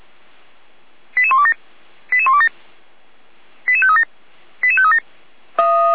Five-tone selective call (ZVEI), heard as quick runs of five tones, tones of about 70 ms. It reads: 88127 88127 88537 88537